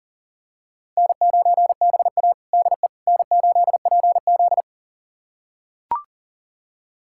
Morse code, heard as a keyed tone, about 37 wpm, 700 Hz. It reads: N9BA DE N8PZ
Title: USA Callsign DE Callsign - 4 Characters - 40wpm